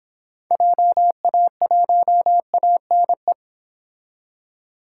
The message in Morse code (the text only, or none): JA1ANE